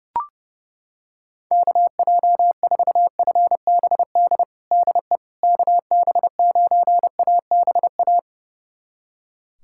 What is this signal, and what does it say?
Morse code, heard as a keyed tone, about 30 wpm, 700 Hz. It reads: KJ4FBD DE KB9ABA
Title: USA Callsign DE Callsign - 6 Characters - 30wpm